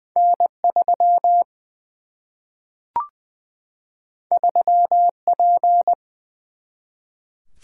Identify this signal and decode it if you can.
Morse code, heard as a keyed tone, about 20 wpm, 700 Hz. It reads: N 3 3 P
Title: Single-Letter Single-Number 20wpm